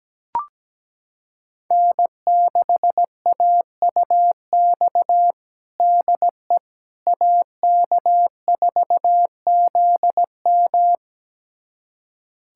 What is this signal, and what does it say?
Morse code, heard as a keyed tone, about 17 wpm, 700 Hz. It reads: N6AUX DE AK4ZM
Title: USA Callsign DE Callsign - 5 Characters - 17wpm